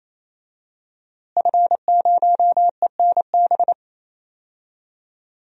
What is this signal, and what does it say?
Morse code, heard as a keyed tone, about 28 wpm, 700 Hz. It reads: F0ENB
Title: DXCC-100 Call Signs - 5 Characters - Repeated 3x - 28wpm